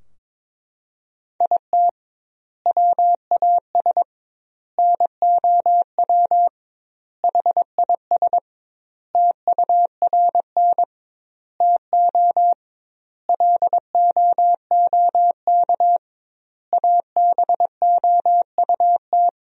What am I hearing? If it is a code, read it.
Morse code, heard as a keyed tone, about 22 wpm, 700 Hz. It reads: IT WAS NOW HIS TURN TO LOOK ABOUT
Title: 2x Word-Spacing - Sentences from Top 200 Words - 22wpm